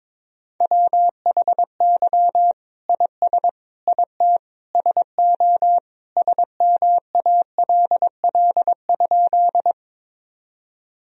Morse code, heard as a keyed tone, about 22 wpm, 700 Hz. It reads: WHY IS IT SO SMALL?
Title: Sentences from Top 1000 Words - Review Part 1 - 22wpm